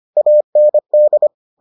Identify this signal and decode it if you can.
Morse code, heard as a keyed tone, about 25 wpm, 600 Hz. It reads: AND